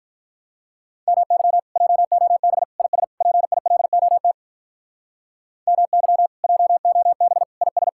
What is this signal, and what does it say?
Morse code, heard as a keyed tone, about 46 wpm, 700 Hz. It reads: MY JOB IS PILOT MY JOB IS